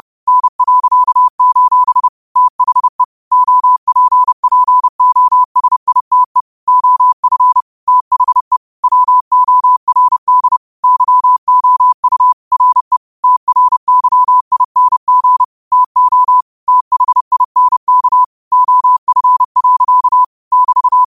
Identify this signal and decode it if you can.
Morse code, heard as a keyed tone, about 30 wpm, 1000 Hz. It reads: N.: THE OPPOSITE OF THE WORD YOU RE TRYING TO THINK OF. <BT>
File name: qotdfiles / 30wpm